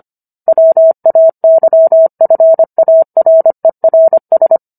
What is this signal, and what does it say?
Morse code, heard as a keyed tone, about 25 wpm, 650 Hz. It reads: WAYFARERS